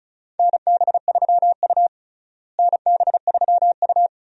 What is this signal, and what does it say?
Morse code, heard as a keyed tone, about 36 wpm, 700 Hz. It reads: NB3U NB3U
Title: USA Call Signs - 4 Characters - Repeated 2x - 35wpm